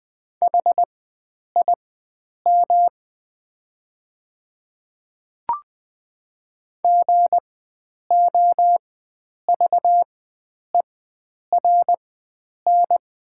Morse code, heard as a keyed tone, about 20 wpm, spaced as for 9 wpm, 700 Hz. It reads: HIM GOVERN
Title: 4x Character Spacing - Top 500 Words - 20wpm